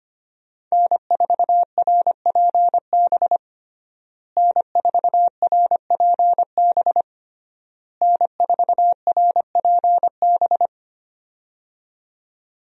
Morse code, heard as a keyed tone, about 25 wpm, 700 Hz. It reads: N4RPB N4RPB N4RPB
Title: USA Call Signs - 5 Characters - Repeated 3x - 25wpm